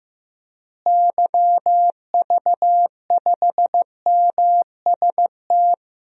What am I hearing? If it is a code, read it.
Morse code, heard as a keyed tone, about 15 wpm, 700 Hz. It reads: YV5MST
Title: DXCC-100 Call Signs - 6 Characters - Repeated 2x - 15wpm